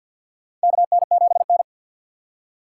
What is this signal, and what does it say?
Morse code, heard as a keyed tone, about 48 wpm, 700 Hz. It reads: KN7N
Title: USA Call Signs - 4 Characters - 3x Character Spacing - 50wpm